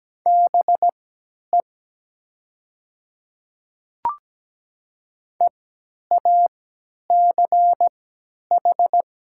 Morse code, heard as a keyed tone, about 17 wpm, spaced as for 10 wpm, 700 Hz. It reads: BE EACH
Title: 3x Character Spacing - Top 100 Words - 17wpm